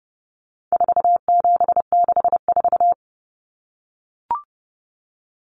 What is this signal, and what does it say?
Morse code, heard as a keyed tone, about 30 wpm, 700 Hz. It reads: 4764
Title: Numbers - 4-Digit 30wpm